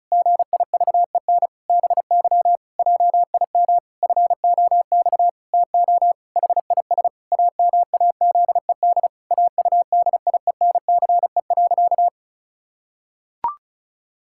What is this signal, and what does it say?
Morse code, heard as a keyed tone, about 34 wpm, 700 Hz. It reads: GIVEN BY JIM FOX TO HIS AMAZED AUDIENCE.
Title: Pangram Challenge v2 - 35wpm